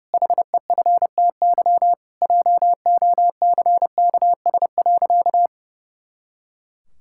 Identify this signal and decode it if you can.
Morse code, heard as a keyed tone, about 30 wpm, 700 Hz. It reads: HEFTY JOCKS.
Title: Pangram Challenge v2 - 30wpm